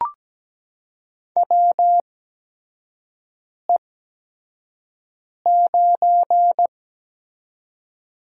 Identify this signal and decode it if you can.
Morse code, heard as a keyed tone, about 17 wpm, spaced as for 5 wpm, 700 Hz. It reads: WE9
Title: USA Call Signs - 4 Characters - 8x Character Spacing - 17wpm